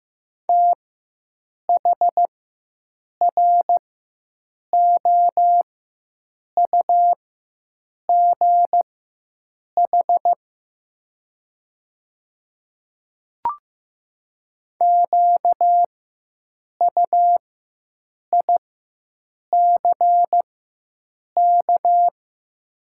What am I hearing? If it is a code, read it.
Morse code, heard as a keyed tone, about 15 wpm, spaced as for 7 wpm, 690 Hz. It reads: THROUGH QUICK
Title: 4x Character Spacing - Top 500 Words - 15wpm